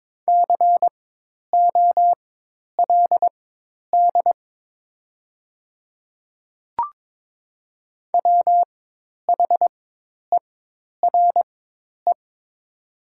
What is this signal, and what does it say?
Morse code, heard as a keyed tone, about 22 wpm, spaced as for 10 wpm, 700 Hz. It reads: COLD WHERE
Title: 4x Character Spacing - Top 500 Words - 22wpm